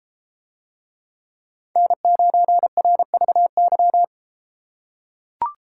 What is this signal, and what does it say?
Morse code, heard as a keyed tone, about 32 wpm, 700 Hz. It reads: N9RVY